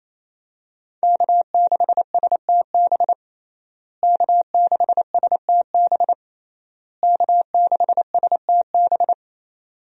Morse code, heard as a keyed tone, about 28 wpm, 700 Hz. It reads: K6STB K6STB K6STB
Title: USA Call Signs - 5 Characters - Repeated 3x - 28wpm